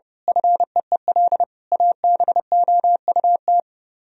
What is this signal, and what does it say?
Morse code, heard as a keyed tone, about 28 wpm, 700 Hz. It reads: FEEL ABOUT
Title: Speed-Racing - Sets of 5 Words 30wpm